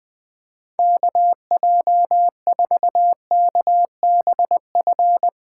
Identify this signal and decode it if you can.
Morse code, heard as a keyed tone, about 20 wpm, 700 Hz. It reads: KJ4KBF
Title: USA Call Signs - 6 Characters - 3x Character Spacing - 20wpm